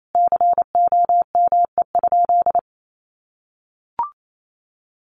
Morse code, heard as a keyed tone, about 28 wpm, 700 Hz. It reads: COME?
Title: Sentences from Top 1400 Words 28wpm